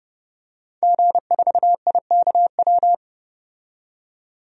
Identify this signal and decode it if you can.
Morse code, heard as a keyed tone, about 32 wpm, 700 Hz. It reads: G4IKW
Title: DXCC-100 Call Signs - 5 Characters - 30wpm